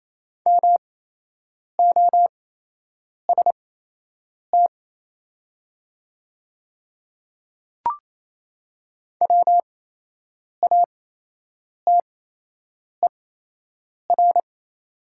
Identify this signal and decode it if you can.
Morse code, heard as a keyed tone, about 28 wpm, spaced as for 8 wpm, 700 Hz. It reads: MOST WATER
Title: 8x Character Spacing - Top 500 Words - 28wpm